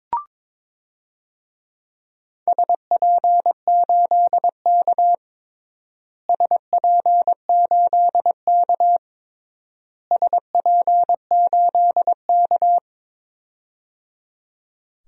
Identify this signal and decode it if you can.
Morse code, heard as a keyed tone, about 21 wpm, 700 Hz. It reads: SP8K SP8K SP8K
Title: DXCC-100 Call Signs - 4 Characters - Repeated 3x - 22wpm